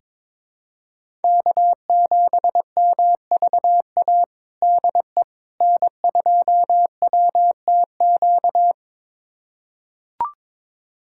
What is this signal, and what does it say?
Morse code, heard as a keyed tone, about 22 wpm, 700 Hz. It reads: K7MVA DE N2WTQ
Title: USA Callsign DE Callsign - 5 Characters - 22wpm